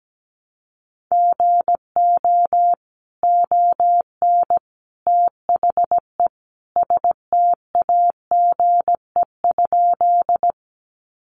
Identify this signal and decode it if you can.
Morse code, heard as a keyed tone, about 17 wpm, 700 Hz. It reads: GO ON THE STAGE?
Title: Sentences from Top 1200 Words 17wpm